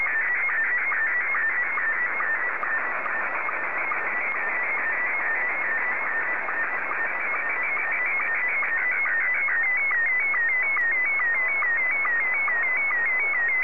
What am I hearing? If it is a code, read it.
SSTV (slow-scan television), Scottie 1.